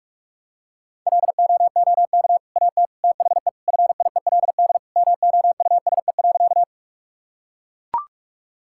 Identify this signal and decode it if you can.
Morse code, heard as a keyed tone, about 42 wpm, 700 Hz. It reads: LOOK AT THE FIELD MOUSE.
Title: Sentences from Top 1000 Words - Review Part 5 - 45wpm